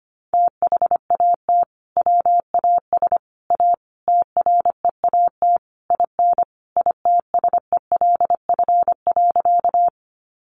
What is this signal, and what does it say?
Morse code, heard as a keyed tone, about 25 wpm, 700 Hz. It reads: THAT WAS A TREAT IN ITSELF.
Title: Sentences from Top 1300 Words 25wpm